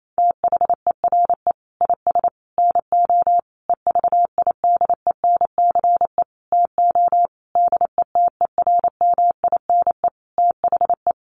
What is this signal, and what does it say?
Morse code, heard as a keyed tone, about 28 wpm, 700 Hz. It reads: THERE IS NO EVIDENCE TO DETERMINE THE
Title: Sentences from Top 1300 Words 28wpm